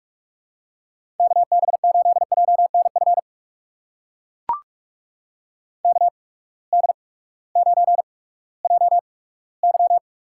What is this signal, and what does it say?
Morse code, heard as a keyed tone, about 44 wpm, spaced as for 20 wpm, 700 Hz. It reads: KB8JNF KD9JY
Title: USA Call Signs - 6 Characters - 8x Character Spacing - 45wpm